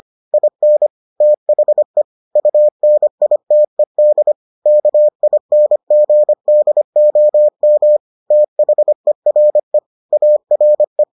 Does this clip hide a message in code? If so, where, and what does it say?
Yes: Morse code from 0.3 to 11.0 s, heard as a keyed tone, about 25 wpm, 600 Hz. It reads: IN THE UNITED KINGDOM THERE ARE